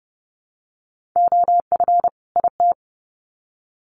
Morse code, heard as a keyed tone, about 30 wpm, 700 Hz. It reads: OF IT
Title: Sets of 2 Words 30wpm